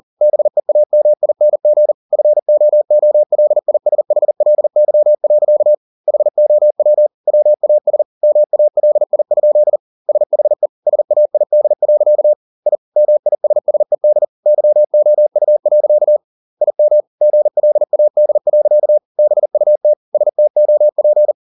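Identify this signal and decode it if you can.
Morse code, heard as a keyed tone, about 40 wpm, 600 Hz. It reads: BEAMING FOOLISHLY. HOW WAS MALI? SHE SAID. I MISSED YOU. IM GLAD. BUT STOP
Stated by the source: Morse code at 40 wpm, practice text, CW med Gnister.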